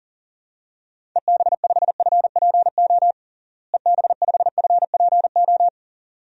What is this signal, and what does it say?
Morse code, heard as a keyed tone, about 37 wpm, 700 Hz. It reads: EB5FPO EB5FPO
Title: DXCC-100 Call Signs - 6 Characters - Repeated 2x - 40wpm